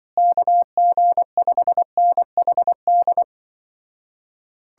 Morse code, heard as a keyed tone, about 24 wpm, 700 Hz. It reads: KG5NHD
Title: Speed-Racing Common QSO Phrases Encore - 20wpm